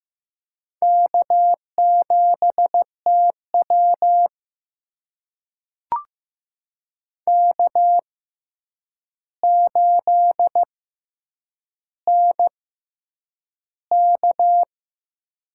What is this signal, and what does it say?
Morse code, heard as a keyed tone, about 15 wpm, spaced as for 8 wpm, 700 Hz. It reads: K7TW K8NK
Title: USA Call Signs - 4 Characters - 6x Character Spacing - 15wpm